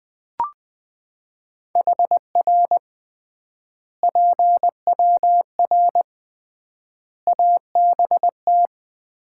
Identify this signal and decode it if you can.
Morse code, heard as a keyed tone, about 19 wpm, 700 Hz. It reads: HR PWR ABT
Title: Common QSO Phrases - 3x Word-Spacing - 20wpm